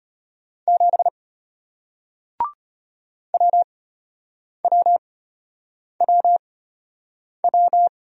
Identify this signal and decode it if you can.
Morse code, heard as a keyed tone, about 33 wpm, spaced as for 8 wpm, 700 Hz. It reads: 7 WWWW